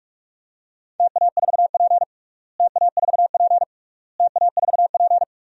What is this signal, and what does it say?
Morse code, heard as a keyed tone, about 46 wpm, 690 Hz. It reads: TA4P TA4P TA4P
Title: DXCC-100 Call Signs - 4 Characters - Repeated 3x - 45wpm